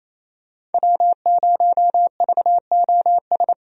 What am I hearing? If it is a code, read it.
Morse code, heard as a keyed tone, about 28 wpm, 700 Hz. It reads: W0VOS
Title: USA Call Signs - 5 Characters - Repeated 3x - 28wpm